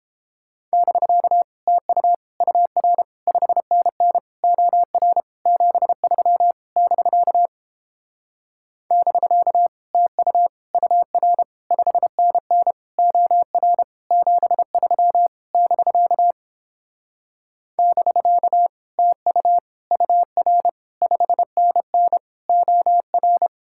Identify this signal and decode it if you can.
Morse code, heard as a keyed tone, about 29 wpm, 700 Hz. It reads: <BK> TU UR 5NN OR 73 <BK> <BK> TU UR 5NN OR 73 <BK> <BK> TU UR 5NN OR